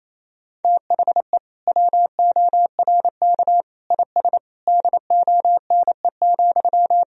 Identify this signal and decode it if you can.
Morse code, heard as a keyed tone, about 28 wpm, 700 Hz. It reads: THE WORK IS DONE,
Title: Sentences from Top 1000 Words - Review Part 4 - 28wpm